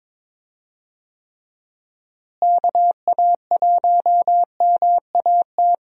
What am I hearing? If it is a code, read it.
Morse code, heard as a keyed tone, about 22 wpm, 700 Hz. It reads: KA1MAT
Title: USA Call Signs - 6 Characters - Repeated 3x - 22wpm